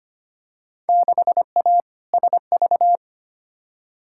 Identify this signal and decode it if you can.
Morse code, heard as a keyed tone, about 25 wpm, 700 Hz. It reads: 6A SV
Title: Field Day Exchanges Encore - 25wpm